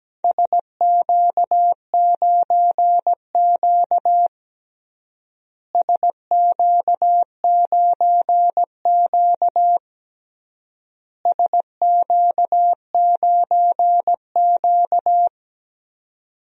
Morse code, heard as a keyed tone, about 17 wpm, 700 Hz. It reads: SQ9Q SQ9Q SQ9Q
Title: DXCC-100 Call Signs - 4 Characters - Repeated 3x - 17wpm